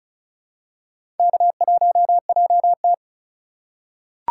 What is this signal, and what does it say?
Morse code, heard as a keyed tone, about 34 wpm, 700 Hz. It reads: K1JT
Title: USA Call Signs - 3x Character Spacing - 35wpm